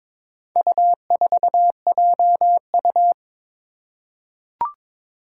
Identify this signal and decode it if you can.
Morse code, heard as a keyed tone, about 22 wpm, 700 Hz. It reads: U4JU
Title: Four Challenging Letter-Number Combinations 22wpm